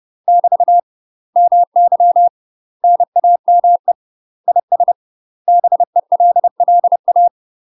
Morse code, heard as a keyed tone, about 30 wpm, 700 Hz. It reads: <BT> MY NAME IS BELLA
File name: UK MASTER QSOs TRAINING / 30WPM / UK MASTER QSOs 0047